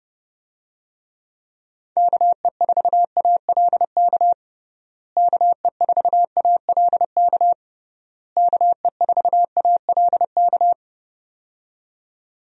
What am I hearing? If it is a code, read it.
Morse code, heard as a keyed tone, about 30 wpm, 700 Hz. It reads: KE4ALK KE4ALK KE4ALK
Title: USA Call Signs - 6 Characters - Repeated 3x - 30wpm